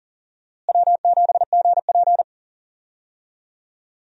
Morse code, heard as a keyed tone, about 38 wpm, 700 Hz. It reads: W7GP
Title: USA Call Signs - 4 Characters - Repeated 3x - 40wpm